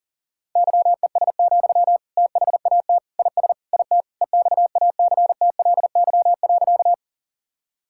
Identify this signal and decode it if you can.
Morse code, heard as a keyed tone, about 40 wpm, 700 Hz. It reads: YES, THAT IS IT EXACTLY.
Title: Sentences from Top 1000 Words - Review Part 6 - 40wpm